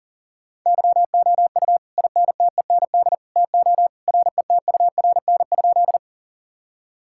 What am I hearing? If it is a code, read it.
Morse code, heard as a keyed tone, about 40 wpm, 700 Hz. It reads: YOU INTEND TO RETURN?
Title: Sentences from Top 1600 Words 40wpm